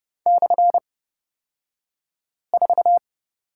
Morse code, heard as a keyed tone, about 30 wpm, spaced as for 5 wpm, 700 Hz. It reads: /4
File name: Lesson 37d - Letter x - Two Characters 30wpm